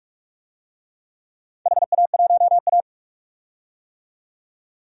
Morse code, heard as a keyed tone, about 46 wpm, 700 Hz. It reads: HA1A
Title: DXCC-100 Call Signs - 4 Characters - 45wpm